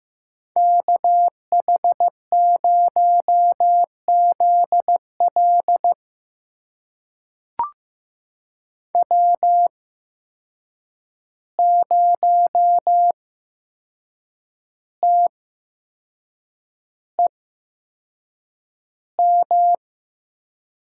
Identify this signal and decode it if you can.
Morse code, heard as a keyed tone, about 15 wpm, spaced as for 6 wpm, 700 Hz. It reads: KH0ZL W0TEM
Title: USA Call Signs - 5 Characters - 8x Character Spacing - 15wpm